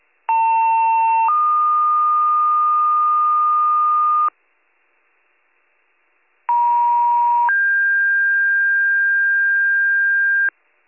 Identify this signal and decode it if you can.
Two-tone page, heard as a one-second tone, then a three-second tone, about 900 then 1220 Hz, about 950 then 1600 Hz. It is heard from 0.3 to 10.5 s.